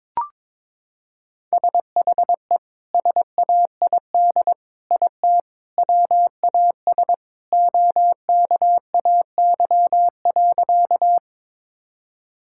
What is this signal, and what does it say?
Morse code, heard as a keyed tone, about 22 wpm, 700 Hz. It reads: SHE SAID IT WAS OKAY.
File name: practice / 22wpm / Sentences with 2-Word Phrases - 22wpm